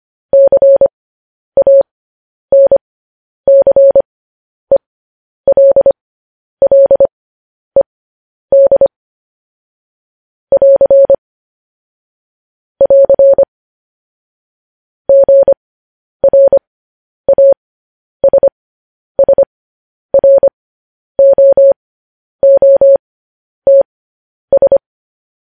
Morse code, heard as a keyed tone, about 25 wpm, spaced as for 10 wpm, 550 Hz. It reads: CANCELLED <AR> <AR> GRASSROOTS
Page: News Headlines in Morse Code at 10 WPM